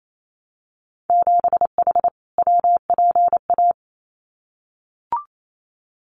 Morse code, heard as a keyed tone, about 28 wpm, 700 Hz. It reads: 7H WPA
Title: Winter Field Day Exchanges Encore 28wpm